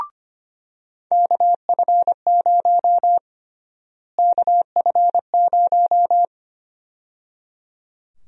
Morse code, heard as a keyed tone, about 25 wpm, 700 Hz. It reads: KF0 KF0
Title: USA Call Sign Prefixes - Repeated 2x - 25wpm